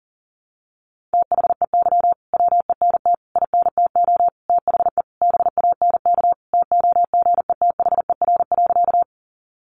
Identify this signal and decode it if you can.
Morse code, heard as a keyed tone, about 37 wpm, 700 Hz. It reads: THEY WENT INTO THE BANK TOGETHER.
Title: Sentences from Top 900 Words 40wpm